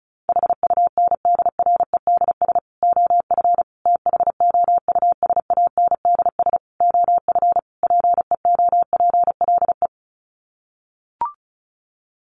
Morse code, heard as a keyed tone, about 34 wpm, 700 Hz. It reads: HUNDREDS OF THOUSANDS OF PEOPLE
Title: Sets of 5 Words Encore2 35wpm